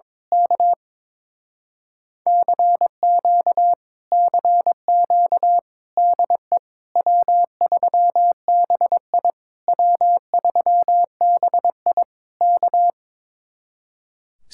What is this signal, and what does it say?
Morse code, heard as a keyed tone, about 22 wpm, 700 Hz. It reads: K CQ CQ DE W3BI W3BI K
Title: Speed-Racing Common QSO Phrases - 22wpm